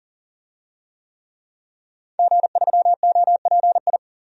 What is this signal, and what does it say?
Morse code, heard as a keyed tone, about 40 wpm, 700 Hz. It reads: G3OPI
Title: DXCC-100 Call Signs - 5 Characters - 40wpm